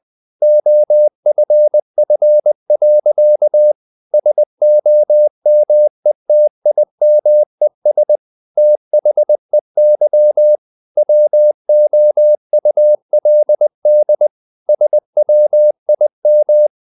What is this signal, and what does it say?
Morse code, heard as a keyed tone, about 20 wpm, 600 Hz. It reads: OFF. SOMETIMES THEY WOULD SWIM
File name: the old man and the sea part 2 20wpm